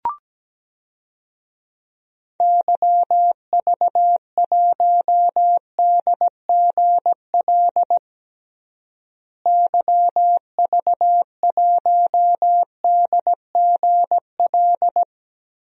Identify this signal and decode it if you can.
Morse code, heard as a keyed tone, about 17 wpm, 700 Hz. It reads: YV1DGL YV1DGL
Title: DXCC-100 Call Signs - 6 Characters - Repeated 3x - 17wpm